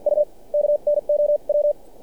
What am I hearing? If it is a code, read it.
Morse code, heard as a keyed tone, about 47 wpm, spaced as for 40 wpm, 600 Hz. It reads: U KNOW
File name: archived / CRSnet